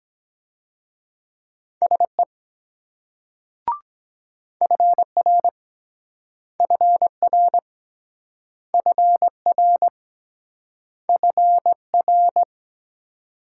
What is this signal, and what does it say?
Morse code, heard as a keyed tone, about 23 wpm, 700 Hz. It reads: SE FR FR FR FR